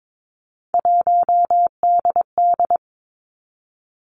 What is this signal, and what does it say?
Morse code, heard as a keyed tone, about 22 wpm, 700 Hz. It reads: 1DD
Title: Three Challenging Letter-Number Combinations 22wpm